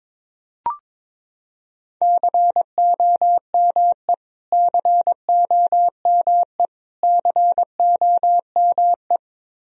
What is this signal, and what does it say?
Morse code, heard as a keyed tone, about 22 wpm, 700 Hz. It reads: COME COME COME
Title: Top 100 Words Repeated 22wpm